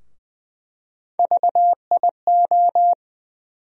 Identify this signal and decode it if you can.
Morse code, heard as a keyed tone, about 20 wpm, 700 Hz. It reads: VIO
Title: USA Call Sign Postfixes - 3 Letters 20wpm